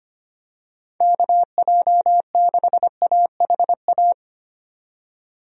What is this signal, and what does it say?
Morse code, heard as a keyed tone, about 25 wpm, 700 Hz. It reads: KJ6AHA
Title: USA Call Signs - 6 Characters - 25wpm